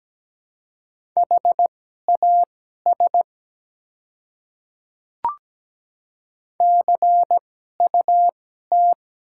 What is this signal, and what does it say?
Morse code, heard as a keyed tone, about 17 wpm, spaced as for 12 wpm, 700 Hz. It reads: HAS CUT
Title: 2x Character Spacing - Top 300 Words - 17wpm